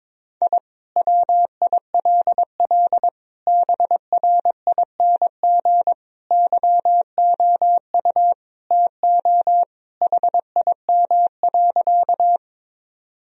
Morse code, heard as a keyed tone, about 22 wpm, 700 Hz. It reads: I WILL BRING YOU TO HIM.
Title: Sentences from Top 1000 Words - Review Part 3 - 22wpm